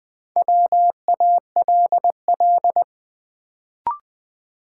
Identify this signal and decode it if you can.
Morse code, heard as a keyed tone, about 20 wpm, 700 Hz. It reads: WALL
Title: Sentences from Top 1100 Words - New Word Drill Alt - 20wpm